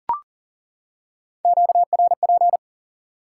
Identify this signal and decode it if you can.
Morse code, heard as a keyed tone, about 40 wpm, 700 Hz. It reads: QRP
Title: Top 90 QSO Elements 40wpm